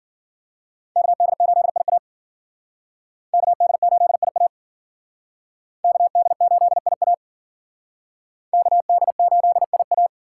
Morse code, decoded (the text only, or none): KD8IA KD8IA KD8IA KD8IA